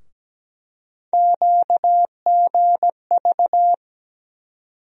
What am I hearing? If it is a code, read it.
Morse code, heard as a keyed tone, about 17 wpm, 700 Hz. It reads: QGV
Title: USA Call Sign Postfixes - 3 Letters 17wpm